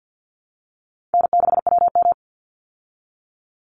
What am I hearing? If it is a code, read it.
Morse code, heard as a keyed tone, about 48 wpm, 700 Hz. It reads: N6WM
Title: Easy Call Signs Repeated 3x - 50wpm